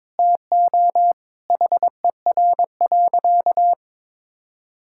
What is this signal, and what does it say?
Morse code, heard as a keyed tone, about 22 wpm, 700 Hz. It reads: TO HER.
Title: Sentences from Top 1000 Words - Review Part 6 - 22wpm